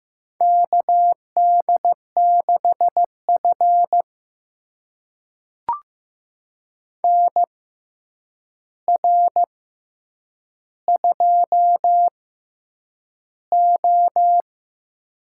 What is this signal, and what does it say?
Morse code, heard as a keyed tone, about 15 wpm, spaced as for 8 wpm, 700 Hz. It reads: KD6F NR2O
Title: USA Call Signs - 4 Characters - 6x Character Spacing - 15wpm